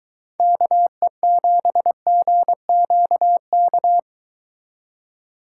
Morse code, heard as a keyed tone, about 22 wpm, 700 Hz. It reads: KE7GQK